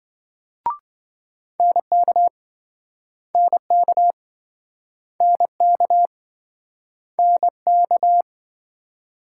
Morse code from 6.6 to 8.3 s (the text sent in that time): NK